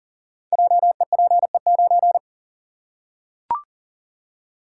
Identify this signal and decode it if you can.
Morse code, heard as a keyed tone, about 40 wpm, 700 Hz. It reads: JEPE9
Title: Code Groups - 5 Letter:Numbers - 40wpm